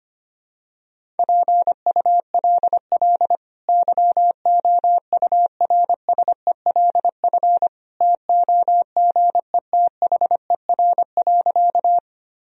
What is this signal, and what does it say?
Morse code, heard as a keyed tone, about 25 wpm, 700 Hz. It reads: PULL YOURSELF TOGETHER.
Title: Sentences from Top 1000 Words - Review Part 2 - 25wpm